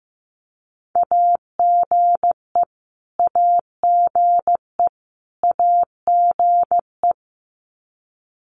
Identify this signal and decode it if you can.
Morse code, heard as a keyed tone, about 15 wpm, 700 Hz. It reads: AGE AGE AGE
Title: Top 500 Words Repeated 15wpm